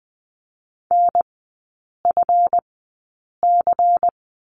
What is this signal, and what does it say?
Morse code, heard as a keyed tone, about 20 wpm, spaced as for 8 wpm, 700 Hz. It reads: NFC
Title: 138 Lesson 23e - Letter f - Three Characters 20wpm